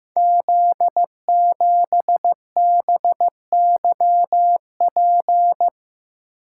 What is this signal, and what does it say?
Morse code, heard as a keyed tone, about 15 wpm, 700 Hz. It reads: Z7BYP
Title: Code Groups - 5 Letter:Numbers - 15wpm